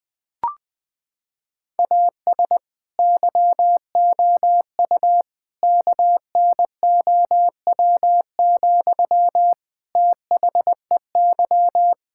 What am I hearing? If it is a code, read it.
Morse code, heard as a keyed tone, about 19 wpm, 700 Hz. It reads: AS YOU KNOW, THEY
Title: Sentences from Top 1000 Words - Review Part 5 - 20wpm